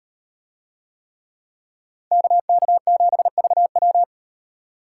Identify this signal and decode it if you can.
Morse code, heard as a keyed tone, about 36 wpm, 700 Hz. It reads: KK7VW